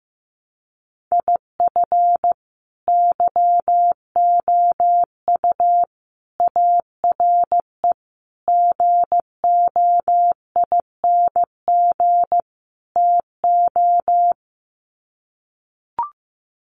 Morse code, heard as a keyed tone, about 15 wpm, 700 Hz. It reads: IF YOU ARE GOING TO
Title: Sets of 5 Words Encore2 15wpm